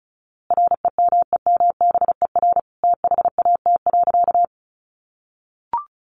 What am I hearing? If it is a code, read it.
Morse code, heard as a keyed tone, about 34 wpm, 700 Hz. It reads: REMEMBER THAT.
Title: A Sound Of Thunder 35wpm